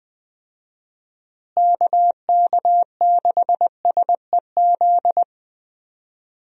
Morse code, heard as a keyed tone, about 20 wpm, 700 Hz. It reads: KK6SEZ
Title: USA Call Signs - 6 Characters - Repeated 2x - 20wpm